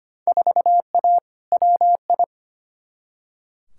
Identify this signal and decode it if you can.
Morse code, heard as a keyed tone, about 25 wpm, 700 Hz. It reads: 4A WI
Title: Field Day Exchanges - 25wpm